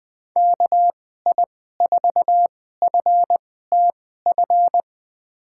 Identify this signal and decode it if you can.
Morse code, heard as a keyed tone, about 20 wpm, spaced as for 14 wpm, 700 Hz. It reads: KI4FTF
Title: USA Call Signs - 2x Character Spacing - 20wpm